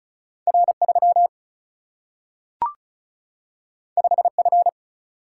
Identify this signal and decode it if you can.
Morse code, heard as a keyed tone, about 36 wpm, 700 Hz. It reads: R 3 5 F
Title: Single-Letter Single-Number 35wpm